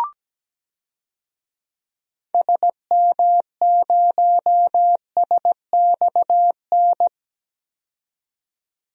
Morse code, heard as a keyed tone, about 17 wpm, 700 Hz. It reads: SM0SXN
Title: DXCC-100 Call Signs - 6 Characters - 17wpm